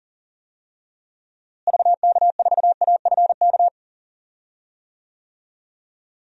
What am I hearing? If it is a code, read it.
Morse code, heard as a keyed tone, about 40 wpm, 700 Hz. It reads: VK4AFK
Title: DXCC-100 Call Signs - 6 Characters - Repeated 3x - 40wpm